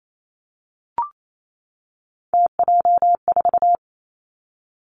Morse code, heard as a keyed tone, about 28 wpm, 700 Hz. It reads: TJ4
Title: Three Challenging Letter-Number Combinations 28wpm